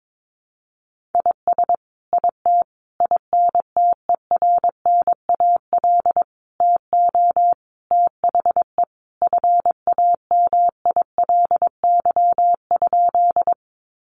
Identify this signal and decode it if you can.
Morse code, heard as a keyed tone, about 22 wpm, 700 Hz. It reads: IS IT INTERNAL TO THE FAMILY?
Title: Sentences from Top 1400 Words 22wpm